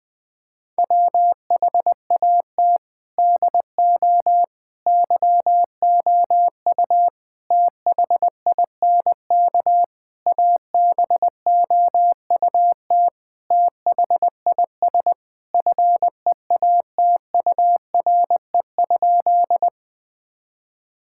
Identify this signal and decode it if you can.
Morse code, heard as a keyed tone, about 20 wpm, 700 Hz. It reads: WHAT DO YOU THINK ABOUT THIS FEATURE?
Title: Sentences from Top 1000 Words - Review Part 5 - 20wpm